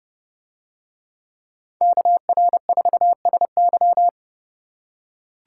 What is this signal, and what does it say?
Morse code, heard as a keyed tone, about 29 wpm, 700 Hz. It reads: KR4SY